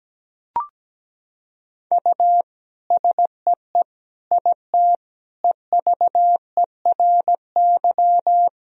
Morse code, heard as a keyed tone, about 17 wpm, 700 Hz. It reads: U SEE IT EVERY
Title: Sentences from Top 200 Words and Abbreviations 17wpm